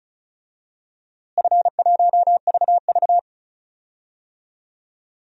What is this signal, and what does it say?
Morse code, heard as a keyed tone, about 36 wpm, 700 Hz. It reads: F1VV
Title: DXCC-100 Call Signs - 4 Characters - Repeated 2x - 35wpm